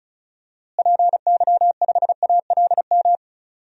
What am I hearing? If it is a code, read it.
Morse code, heard as a keyed tone, about 36 wpm, 700 Hz. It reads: PY5ALM
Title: DXCC-100 Call Signs - 6 Characters - Repeated 3x - 35wpm